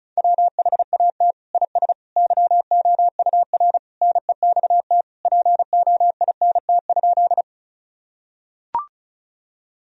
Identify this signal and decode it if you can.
Morse code, heard as a keyed tone, about 34 wpm, 700 Hz. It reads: WHAT IS YOUR NEXT POINT?
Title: Sentences from Top 1000 Words - Review Part 1 - 35wpm